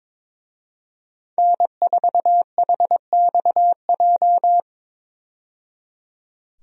Morse code, heard as a keyed tone, about 22 wpm, 700 Hz. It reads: N4HXJ
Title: USA Call Signs - 5 Characters - 22wpm